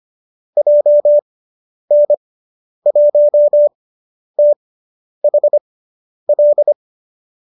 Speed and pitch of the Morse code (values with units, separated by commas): 25 wpm, spaced as for 10 wpm, 590 Hz